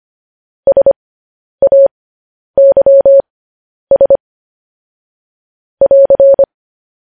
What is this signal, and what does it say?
Morse code, heard as a keyed tone, about 25 wpm, spaced as for 10 wpm, 550 Hz. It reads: SAYS <AR>
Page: News Headlines in Morse Code at 10 WPM